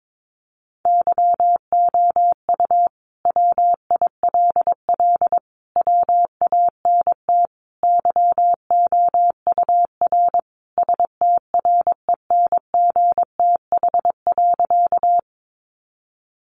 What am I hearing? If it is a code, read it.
Morse code, heard as a keyed tone, about 22 wpm, 700 Hz. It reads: YOU WILL WANT YOUR STRENGTH.
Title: Sentences from Top 1300 Words 22wpm